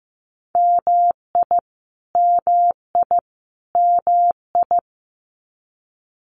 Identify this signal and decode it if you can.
Morse code, heard as a keyed tone, about 15 wpm, 700 Hz. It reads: MI MI MI
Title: US States Repeated 3x 15wpm